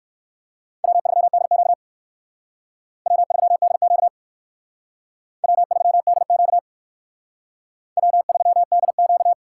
Morse code, heard as a keyed tone, about 57 wpm, 700 Hz. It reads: W3DQ W3DQ W3DQ W3DQ